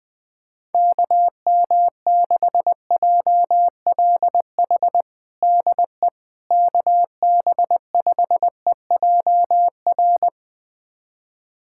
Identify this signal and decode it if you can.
Morse code, heard as a keyed tone, about 20 wpm, 700 Hz. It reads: KM6JLH DE KB5EJR
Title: USA Callsign DE Callsign - 6 Characters - 20wpm